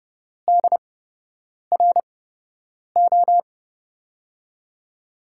Morse code, heard as a keyed tone, about 30 wpm, spaced as for 8 wpm, 700 Hz. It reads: DRO
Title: USA Call Signs - 6 Characters - 8x Character Spacing - 30wpm